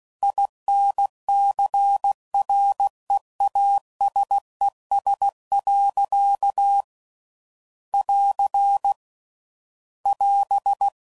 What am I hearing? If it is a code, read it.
Morse code, heard as a keyed tone, about 16 wpm, 800 Hz. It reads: INCREASES. <AR> <AS>
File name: CW Code Groups @15WPM/Technical Subjects